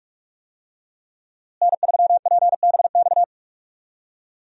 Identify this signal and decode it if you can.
Morse code, heard as a keyed tone, about 44 wpm, 700 Hz. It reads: N3PBX
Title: USA Call Signs - 5 Characters - Repeated 2x - 45wpm